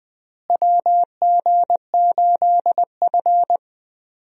Morse code, heard as a keyed tone, about 20 wpm, 700 Hz. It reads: WG8F
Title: USA Call Signs - 4 Characters - Repeated 2x - 20wpm